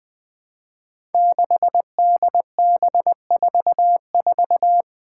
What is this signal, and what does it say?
Morse code, heard as a keyed tone, about 21 wpm, 700 Hz. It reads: 6DB44
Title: Five Challenging Letter-Number Combinations 20wpm